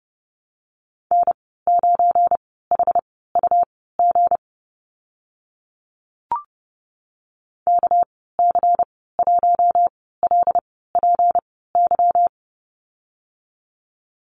Morse code, heard as a keyed tone, about 30 wpm, spaced as for 17 wpm, 700 Hz. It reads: N9HUG KC1LPY
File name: Lesson 37g - Letter x - Call Signs 30wpm